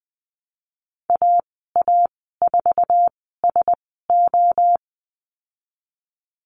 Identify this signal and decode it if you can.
Morse code, heard as a keyed tone, about 20 wpm, spaced as for 15 wpm, 700 Hz. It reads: AA4SO
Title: 048 Lesson 9g - Number 4 - Call Signs 20wpm